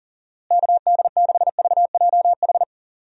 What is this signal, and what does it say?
Morse code, heard as a keyed tone, about 38 wpm, 700 Hz. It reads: KD6VJH
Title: USA Call Signs - 6 Characters - Repeated 2x - 40wpm